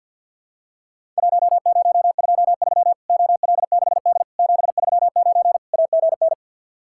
Morse code, heard as a keyed tone, about 50 wpm, 700 Hz. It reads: 1023 OL6D 739 AGN
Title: Worked All Europe DX Contest - QTC Exchange with Repeats - 50wpm